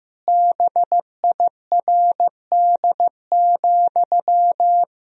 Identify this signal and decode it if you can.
Morse code, heard as a keyed tone, about 16 wpm, 700 Hz. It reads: BIRD,
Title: Sentences from Top 1000 Words - Review Part 2 - 15wpm